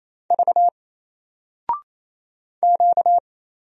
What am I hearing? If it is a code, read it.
Morse code, heard as a keyed tone, about 28 wpm, 700 Hz. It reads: V Q
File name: Single-Letter-Number-28wpm